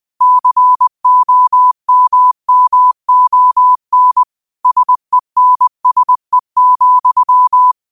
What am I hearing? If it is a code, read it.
Morse code, heard as a keyed tone, about 20 wpm, 1000 Hz. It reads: COMMON SENSE,
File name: qotdfiles / 20wpm